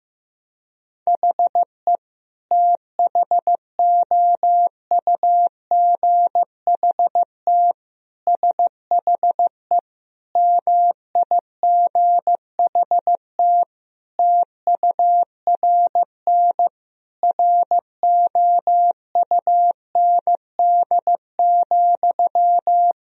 Morse code, heard as a keyed tone, about 15 wpm, 700 Hz. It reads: HE THOUGHT SHE MIGHT TURN ROUND,
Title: Sentences from Top 1000 Words - Review Part 1 - 15wpm